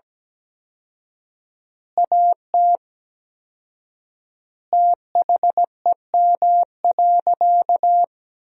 Morse code, heard as a keyed tone, about 17 wpm, 700 Hz. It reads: AT THEM.
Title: 4x Word-Spacing - Sentences from Top 100 Words - 17wpm